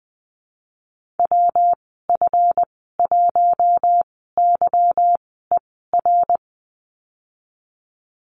Morse code, heard as a keyed tone, about 20 wpm, spaced as for 15 wpm, 700 Hz. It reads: WF1YER
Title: 161 Lesson 26g - Letter g - Call Signs 20wpm